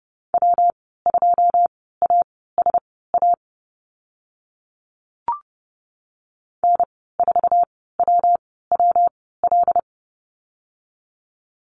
Morse code, heard as a keyed tone, about 30 wpm, spaced as for 17 wpm, 700 Hz. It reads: W2ASA N4WWL
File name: Lesson 19g - Letter w - Call Signs 30wpm